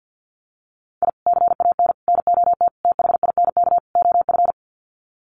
Morse code, heard as a keyed tone, about 50 wpm, 700 Hz. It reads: I CAN NOT THINK OF
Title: Sets of 5 Words 50wpm